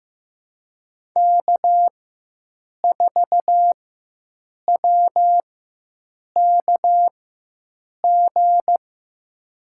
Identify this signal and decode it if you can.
Morse code, heard as a keyed tone, about 15 wpm, spaced as for 7 wpm, 700 Hz. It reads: K4WKG
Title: USA Call Signs - 5 Characters - 4x Character Spacing - 15wpm